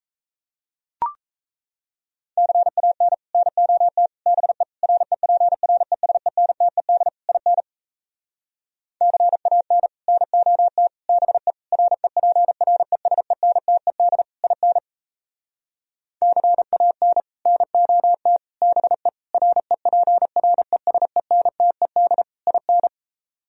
Morse code, heard as a keyed tone, about 37 wpm, 700 Hz. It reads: CAN NOT BE REPRESENTED IN CAN NOT BE REPRESENTED IN CAN NOT BE REPRESENTED IN